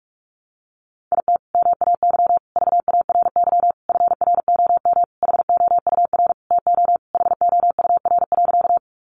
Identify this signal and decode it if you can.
Morse code, heard as a keyed tone, about 44 wpm, 700 Hz. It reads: IT MAY VARY FROM HOUR TO HOUR.
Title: Sentences from Top 1200 Words 45wpm